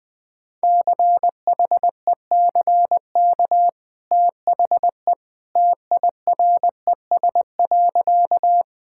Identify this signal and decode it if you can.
Morse code, heard as a keyed tone, about 20 wpm, 700 Hz. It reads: CHECK THE TIRES.
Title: Easy Sentences with Hints - 20wpm